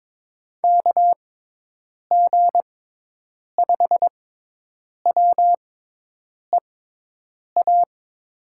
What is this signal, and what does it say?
Morse code, heard as a keyed tone, about 22 wpm, spaced as for 8 wpm, 700 Hz. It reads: KG5WEA
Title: USA Call Signs - 6 Characters - 6x Character Spacing - 22wpm